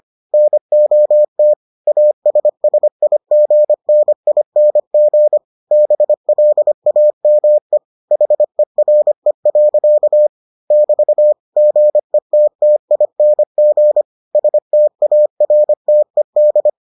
Morse code, heard as a keyed tone, about 25 wpm, 600 Hz. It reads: NOT ASSIGNING BLAME HERE. <BT> GETTING STARTED